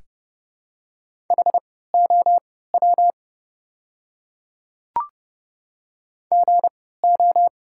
Morse code, heard as a keyed tone, about 30 wpm, spaced as for 17 wpm, 700 Hz. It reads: HOW GO
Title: 3x Character Spacing - Top 100 Words - 30wpm